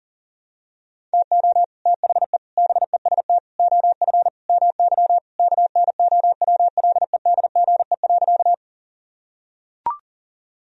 Morse code, heard as a keyed tone, about 40 wpm, 700 Hz. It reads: TO THE BEST OF MY KNOWLEDGE.
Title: Sentences from Top 1000 Words - Review Part 3 - 40wpm